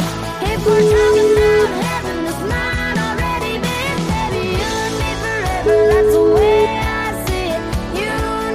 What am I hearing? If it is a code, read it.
Ringback tone (Europe, 425 Hz), with noise.